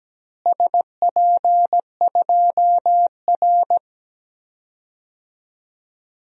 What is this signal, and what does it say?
Morse code, heard as a keyed tone, about 17 wpm, 700 Hz. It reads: SP2R
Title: DXCC-100 Call Signs - 4 Characters - Repeated 2x - 17wpm